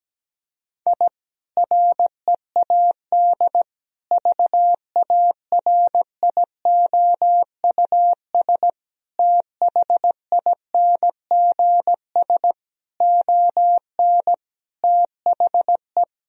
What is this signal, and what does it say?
Morse code, heard as a keyed tone, about 17 wpm, 700 Hz. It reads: I READ VARIOUS THINGS ON THE
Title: Sentences from Top 1000 Words - Review Part 4 - 17wpm